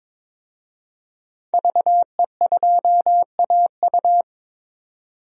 Morse code, heard as a keyed tone, about 22 wpm, 700 Hz. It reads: VE2AU
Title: DXCC-100 Call Signs - 5 Characters - 22wpm